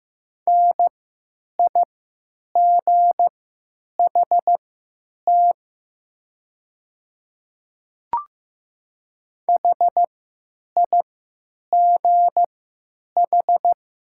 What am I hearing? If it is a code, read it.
Morse code, heard as a keyed tone, about 15 wpm, spaced as for 9 wpm, 700 Hz. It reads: NIGHT HIGH
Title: 3x Character Spacing - Top 300 Words - 15wpm